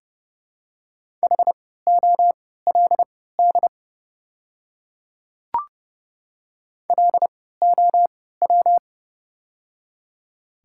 Morse code, heard as a keyed tone, about 30 wpm, spaced as for 17 wpm, 700 Hz. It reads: HOLD LOW
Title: 3x Character Spacing - Top 500 Words - 30wpm